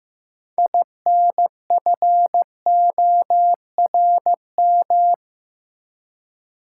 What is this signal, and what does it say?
Morse code, heard as a keyed tone, about 15 wpm, 700 Hz. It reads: INFORM
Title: Six Letter Words - 15wpm